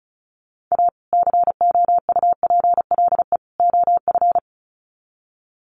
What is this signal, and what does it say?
Morse code, heard as a keyed tone, about 34 wpm, 700 Hz. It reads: A COUPLE OF
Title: Sets of 3 Words 35wpm